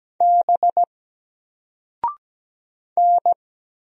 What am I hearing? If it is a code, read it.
Morse code, heard as a keyed tone, about 17 wpm, 700 Hz. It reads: B N
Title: Single Character (Letter:Number:Punctuation:Prosign) 17wpm